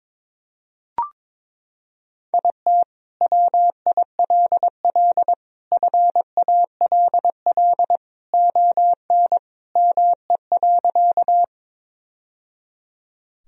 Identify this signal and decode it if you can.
Morse code, heard as a keyed tone, about 22 wpm, 700 Hz. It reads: IT WILL FALL ON ME.
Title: Sentences from Top 1000 Words - Review Part 2 - 22wpm